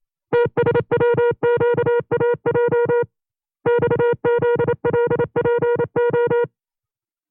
Morse code, heard as a keyed tone, about 27 wpm, 480 Hz. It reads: TSWQAJ XZLPO
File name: CW-practice/25WPM